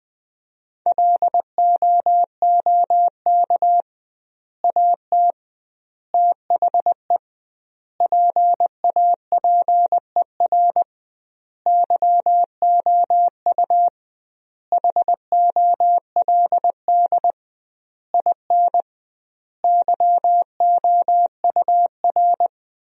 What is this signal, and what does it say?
Morse code, heard as a keyed tone, about 20 wpm, 700 Hz. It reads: LOOK AT THE PAPER YOU HOLD IN YOUR
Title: 2x Word-Spacing - Sentences from Top 400 Words - 20wpm